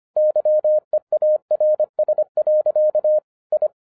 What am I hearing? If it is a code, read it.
Morse code, heard as a keyed tone, about 25 wpm, 610 Hz. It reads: YEARS. I
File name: FIRST FISTS CONTACT-25wpm